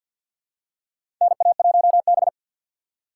Morse code, heard as a keyed tone, about 48 wpm, 690 Hz. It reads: NA1B
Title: USA Call Signs - 4 Characters - 6x Character Spacing - 50wpm